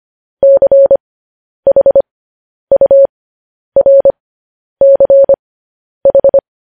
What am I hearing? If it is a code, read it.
Morse code, heard as a keyed tone, about 25 wpm, spaced as for 10 wpm, 550 Hz. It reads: CHURCH